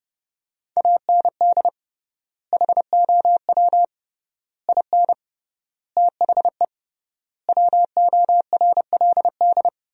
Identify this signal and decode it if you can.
Morse code, heard as a keyed tone, about 30 wpm, 700 Hz. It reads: AND HOW IN THE WORLD
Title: 3x Word-Spacing - Sentences from Top 200 Words - 30wpm